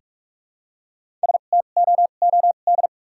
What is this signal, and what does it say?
Morse code, heard as a keyed tone, about 44 wpm, spaced as for 32 wpm, 700 Hz. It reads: STOOD
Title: 2x Character Spacing - Top 500 Words - 45wpm